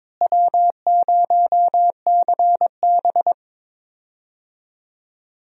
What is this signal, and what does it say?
Morse code, heard as a keyed tone, about 22 wpm, 700 Hz. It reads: W0CB
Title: USA Call Signs - 4 Characters - Repeated 3x - 22wpm